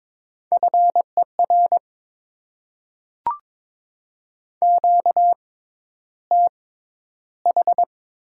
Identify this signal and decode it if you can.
Morse code, heard as a keyed tone, about 22 wpm, spaced as for 11 wpm, 700 Hz. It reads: FER QTH
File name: Top 20 QSO Elements - 6x Character Spacing - 22wpm